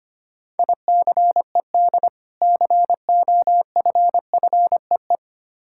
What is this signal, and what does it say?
Morse code, heard as a keyed tone, about 25 wpm, 700 Hz. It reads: ICED COFFEE
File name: practice / 25wpm / Difficult Words with Hints - 25wpm